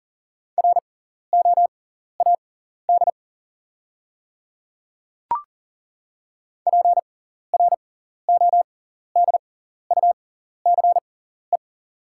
Morse code, heard as a keyed tone, about 37 wpm, spaced as for 14 wpm, 690 Hz. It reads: ROAD PRODUCE
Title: 6x Character Spacing - Top 500 Words - 40wpm